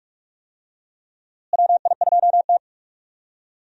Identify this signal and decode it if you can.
Morse code, heard as a keyed tone, about 46 wpm, 700 Hz. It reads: WI2T
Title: USA Call Signs - 4 Characters - Repeated 3x - 45wpm